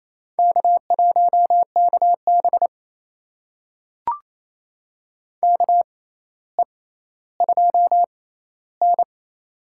Morse code, heard as a keyed tone, about 28 wpm, spaced as for 14 wpm, 700 Hz. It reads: K1KB KE2N
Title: USA Call Signs - 4 Characters - 6x Character Spacing - 28wpm